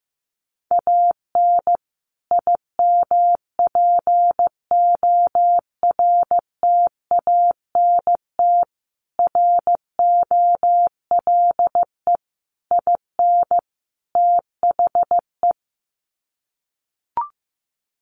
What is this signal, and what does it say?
Morse code, heard as a keyed tone, about 15 wpm, 700 Hz. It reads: AN IMPORTANT ROLE IN THE
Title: Sets of 5 Words Encore2 15wpm